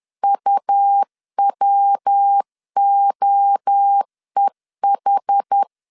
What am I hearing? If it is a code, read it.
Morse code, heard as a keyed tone, about 10 wpm, 800 Hz. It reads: UWOEH
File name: CW / DET / 1 3 1500C S C 10wpm 3335